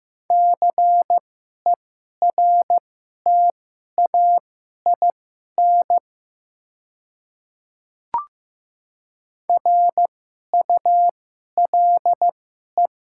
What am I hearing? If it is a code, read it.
Morse code, heard as a keyed tone, about 15 wpm, spaced as for 11 wpm, 700 Hz. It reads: CERTAIN RULE
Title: 2x Character Spacing - Top 500 Words - 15wpm